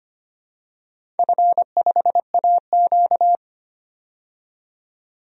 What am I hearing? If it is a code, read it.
Morse code, heard as a keyed tone, about 25 wpm, 700 Hz. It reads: F5AQ
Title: DXCC-100 Call Signs - 4 Characters - 25wpm